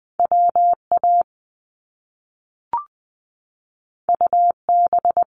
Morse code, heard as a keyed tone, about 20 wpm, 700 Hz. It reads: WA UB
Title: Top Two Letter Combinations in Words 20wpm